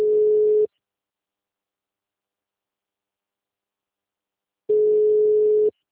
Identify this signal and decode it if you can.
Ringback tone (Europe, 425 Hz).